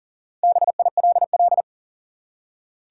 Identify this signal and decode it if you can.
Morse code, heard as a keyed tone, about 40 wpm, 700 Hz. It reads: BILL
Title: Four Letter Words - 40wpm